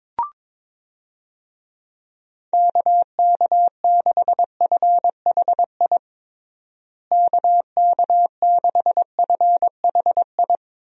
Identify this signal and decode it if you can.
Morse code, heard as a keyed tone, about 22 wpm, 700 Hz. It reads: KK6FHI KK6FHI
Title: USA Call Signs - 6 Characters - Repeated 2x - 22wpm